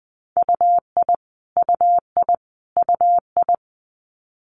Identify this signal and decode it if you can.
Morse code, heard as a keyed tone, about 20 wpm, 700 Hz. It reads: UI UI UI
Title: Multi-letter Phonemes Repeated 3x 20wpm